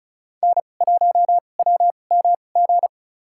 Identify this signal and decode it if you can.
Morse code, heard as a keyed tone, about 36 wpm, spaced as for 25 wpm, 690 Hz. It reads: N1WMG
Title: USA Call Signs - 5 Characters - 2x Character Spacing - 35wpm